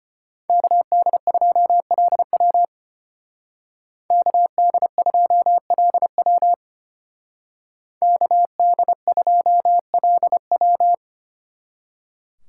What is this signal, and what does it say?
Morse code, heard as a keyed tone, about 33 wpm, 700 Hz. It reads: KD2LW KD2LW KD2LW